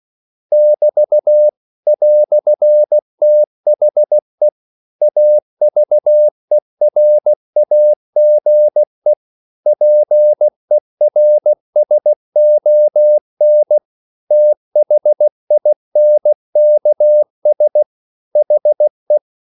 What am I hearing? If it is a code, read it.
Morse code, heard as a keyed tone, about 16 wpm, 600 Hz. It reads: <BT> "THE AVERAGE PERSON THINKS HE
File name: CW / Practice / EnglishQuotations0416